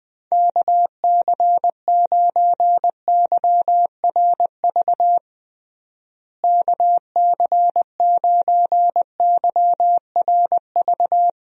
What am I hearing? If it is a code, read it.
Morse code, heard as a keyed tone, about 20 wpm, 700 Hz. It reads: KC9YRV KC9YRV
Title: USA Call Signs - 6 Characters - Repeated 2x - 20wpm